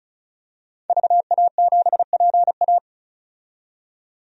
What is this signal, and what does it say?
Morse code, heard as a keyed tone, about 36 wpm, 700 Hz. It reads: VA7PA
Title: DXCC-100 Call Signs - 5 Characters - 35wpm